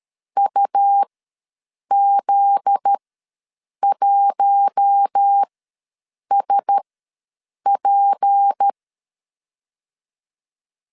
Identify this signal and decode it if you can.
Morse code, heard as a keyed tone, about 13 wpm, spaced as for 7 wpm, 800 Hz. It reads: UZ1SP
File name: LEZIONI / 8 YJ45 / 8 YJ45 ws 12wpm 3146